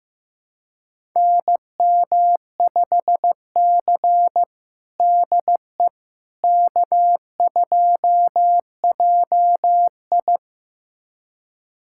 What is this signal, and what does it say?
Morse code, heard as a keyed tone, about 16 wpm, 700 Hz. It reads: NM5C DE K2JI
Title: USA Callsign DE Callsign - 4 Characters - 15wpm